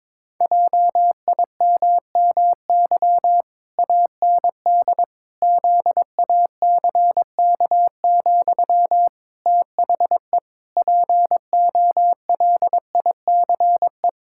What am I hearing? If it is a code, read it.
Morse code, heard as a keyed tone, about 22 wpm, 700 Hz. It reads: JIMMY AND ZACK, THE POLICE
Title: Pangram Challenge v2 - 22wpm